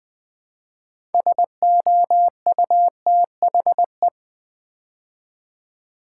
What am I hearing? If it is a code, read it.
Morse code, heard as a keyed tone, about 20 wpm, 700 Hz. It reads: SOUTHE
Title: Build A Word - 2 Letters At A Time - 8-Letter Words - 20wpm